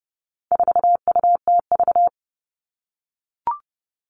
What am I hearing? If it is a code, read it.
Morse code, heard as a keyed tone, about 30 wpm, 700 Hz. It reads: 4UTV
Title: All Cut Numbers - 4-Digit 30wpm